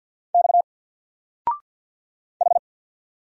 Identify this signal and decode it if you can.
Morse code, heard as a keyed tone, about 49 wpm, 700 Hz. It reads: X H
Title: Single Letter-Number 50wpm